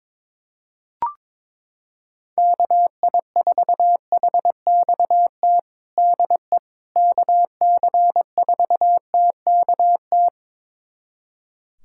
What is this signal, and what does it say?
Morse code, heard as a keyed tone, about 22 wpm, 700 Hz. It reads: KI4HXT DE KC4TKT
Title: USA Callsign DE Callsign - 6 Characters - 22wpm